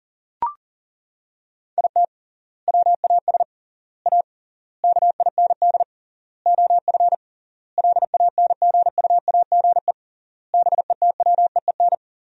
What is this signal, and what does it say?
Morse code, heard as a keyed tone, about 40 wpm, 700 Hz. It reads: IT WAS A KIND OF LANGUAGE BETWEEN
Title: 3x Word-Spacing - Sentences from Top 500 Words - 40wpm